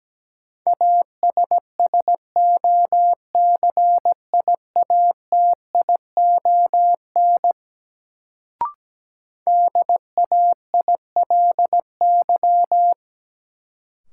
Morse code, heard as a keyed tone, about 17 wpm, 700 Hz. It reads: ASSOCIATION DAILY
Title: Sentences from Top 1100 Words - New Word Drill Alt - 17wpm